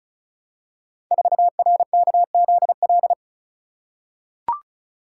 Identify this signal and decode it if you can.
Morse code, heard as a keyed tone, about 36 wpm, 700 Hz. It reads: 4RKZL
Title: Code Groups - 5 Letter:Numbers - 35wpm